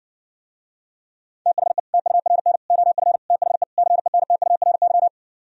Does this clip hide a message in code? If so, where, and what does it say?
Yes: Morse code from 1.5 to 5.1 s, heard as a keyed tone, about 68 wpm, 700 Hz. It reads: THE TURN OF THE CENTURY